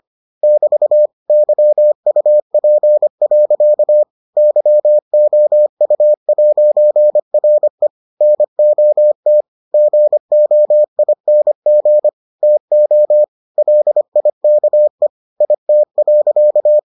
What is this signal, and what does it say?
Morse code, heard as a keyed tone, about 25 wpm, 600 Hz. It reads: <BT> YUP. YOU'RE NOT GOING TO LIKE IT.